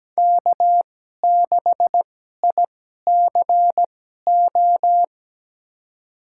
Morse code, heard as a keyed tone, about 17 wpm, spaced as for 12 wpm, 700 Hz. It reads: K6ICO
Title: USA Call Signs - 5 Characters - 2x Character Spacing - 17wpm